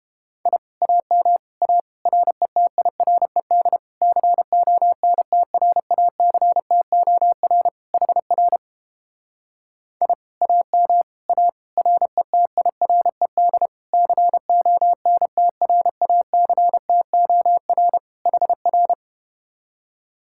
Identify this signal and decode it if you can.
Morse code, heard as a keyed tone, about 30 wpm, 700 Hz. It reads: I AM A RETIRED CONTRACTOR HR I AM A RETIRED CONTRACTOR HR